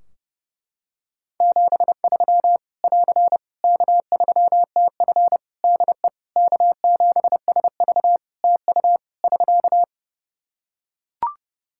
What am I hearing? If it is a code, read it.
Morse code, heard as a keyed tone, about 30 wpm, 700 Hz. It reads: 73 <AR> K3TF DE K7SV TU <SK>
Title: Practice QSO - Exchange 4 - Ending - 30wpm